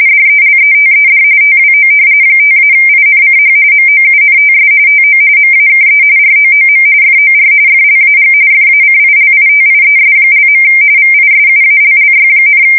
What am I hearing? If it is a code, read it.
RTTY, about 45.45 baud. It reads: DELETED TWO 5-INCH MOUNTS TO MAKE ROOM FOR FLAG FACILITIES). VISUALLY,
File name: fsk170 / battleships